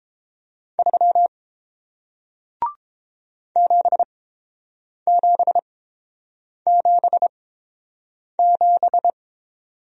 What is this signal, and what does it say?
Morse code, heard as a keyed tone, about 30 wpm, spaced as for 7 wpm, 700 Hz. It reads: 3 7777